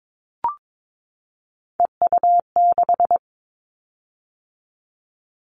Morse code, heard as a keyed tone, about 22 wpm, 700 Hz. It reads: EU6
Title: Callsign Prefixes for Top 50 DXCC - 22wpm